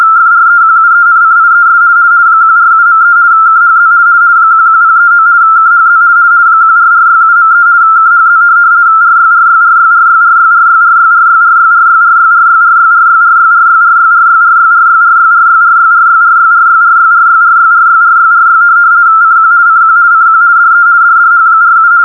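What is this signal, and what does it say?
RTTY, about 50 baud. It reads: FREQUENCIES 147.3 KHZ 11039 KHZ 14467.3 KHZ RYRYRYRYRYRYRYRYRYRYRYRYRYRYRYRYRYRYRYRYRYRYRYRYRYRYRYRYRYRYRYRY CQ CQ CQ DE DDH47
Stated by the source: RTTY test tape from Hamburg Meteo (DWD, Deutscher Wetter Dienst) telegraphic weather service on 147.3, recorded from longwave with only 85 Hz shift.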